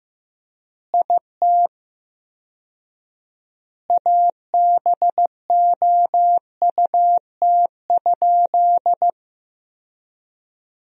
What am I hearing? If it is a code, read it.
Morse code, heard as a keyed tone, about 15 wpm, 700 Hz. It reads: IT ABOUT?
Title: 4x Word-Spacing - Sentences from Top 100 Words - 15wpm